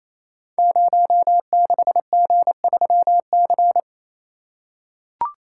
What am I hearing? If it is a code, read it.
Morse code, heard as a keyed tone, about 27 wpm, 700 Hz. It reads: 06G3C
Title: Code Groups - 5 Letter:Numbers - 28wpm